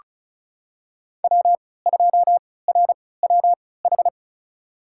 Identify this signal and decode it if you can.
Morse code, heard as a keyed tone, about 34 wpm, spaced as for 20 wpm, 700 Hz. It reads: W2RWH
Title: USA Call Signs - 5 Characters - 3x Character Spacing - 35wpm